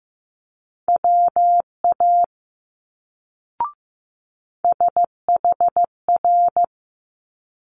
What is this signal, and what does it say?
Morse code, heard as a keyed tone, about 15 wpm, 700 Hz. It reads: WA SHR
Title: Multi-letter Phonemes Quick Fire 15wpm